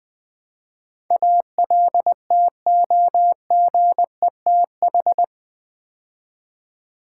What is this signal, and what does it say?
Morse code, heard as a keyed tone, about 20 wpm, 700 Hz. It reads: ALTOGETH
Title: Build A Word - 4 Letters At A Time - 10-Letter Words - 20wpm